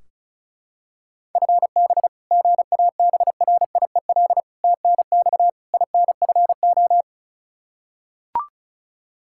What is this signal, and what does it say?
Morse code, heard as a keyed tone, about 36 wpm, 700 Hz. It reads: FB GABRIEL TNX INFO
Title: Practice QSO - Exchange 3 - More Info - 35wpm